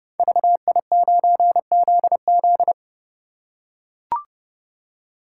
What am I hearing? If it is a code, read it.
Morse code, heard as a keyed tone, about 30 wpm, 700 Hz. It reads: VI9ZZ
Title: Code Groups - 5 Letter:Numbers - 30wpm